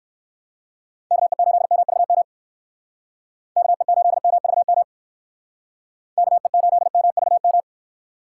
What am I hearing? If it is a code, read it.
Morse code, heard as a keyed tone, about 65 wpm, 700 Hz. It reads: KE8MVM KE8MVM KE8MVM